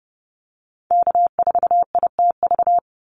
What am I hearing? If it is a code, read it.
Morse code, heard as a keyed tone, about 30 wpm, 700 Hz. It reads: K4ITV
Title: Call Signs Repeated 3x - 30wpm